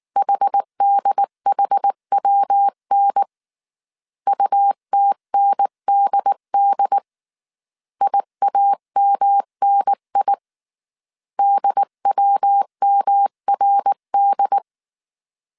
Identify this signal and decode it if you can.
Morse code, heard as a keyed tone, about 19 wpm, 800 Hz. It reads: HDHWN UTNDD IAMNI DWMRD
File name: CW / DET / 1 3 1500C S B 18wpm 1916